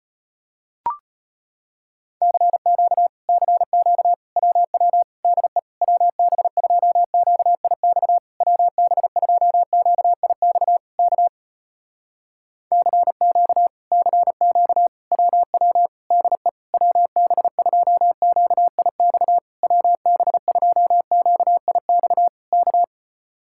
Morse code, heard as a keyed tone, about 35 wpm, 700 Hz. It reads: CQ CQ WW DE WB2QIX WB2QIX K CQ CQ WW DE WB2QIX WB2QIX K